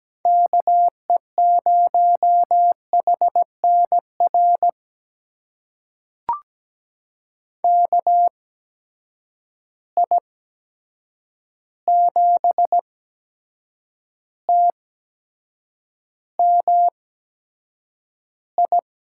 Morse code, heard as a keyed tone, about 17 wpm, spaced as for 7 wpm, 700 Hz. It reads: KE0HNR KI7TMI
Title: USA Call Signs - 6 Characters - 8x Character Spacing - 17wpm